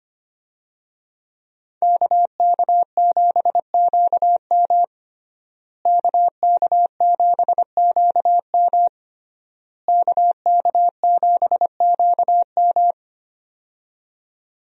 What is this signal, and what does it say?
Morse code, heard as a keyed tone, about 25 wpm, 700 Hz. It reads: KK7QM KK7QM KK7QM
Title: USA Call Signs - 5 Characters - Repeated 3x - 25wpm